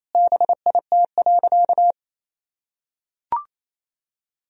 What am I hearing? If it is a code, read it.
Morse code, heard as a keyed tone, about 28 wpm, 700 Hz. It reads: BIT.
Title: Sentences from Top 1000 Words - Review Part 1 - 28wpm